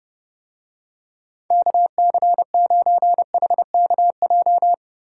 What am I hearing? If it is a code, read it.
Morse code, heard as a keyed tone, about 30 wpm, 700 Hz. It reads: KC9HKJ